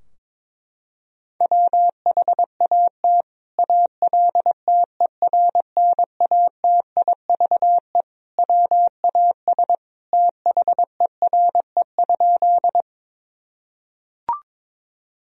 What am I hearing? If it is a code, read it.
Morse code, heard as a keyed tone, about 22 wpm, 700 Hz. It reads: WHAT ALTERNATIVE WAS THERE?
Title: Sentences from Top 1000 Words - Review Part 4 - 22wpm